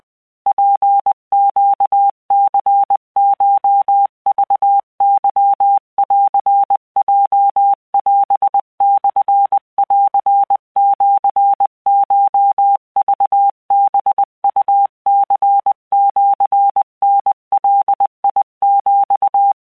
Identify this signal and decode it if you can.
Morse code (Japanese Wabun code), heard as a keyed tone, about 20 wpm, 800 Hz. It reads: ツネニコクケンヲオモンシコクハウニシタガヒ
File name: keuikuchiyokugo20wptan